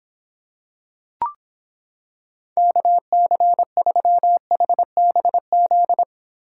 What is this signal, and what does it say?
Morse code, heard as a keyed tone, about 26 wpm, 700 Hz. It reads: KC3HBZ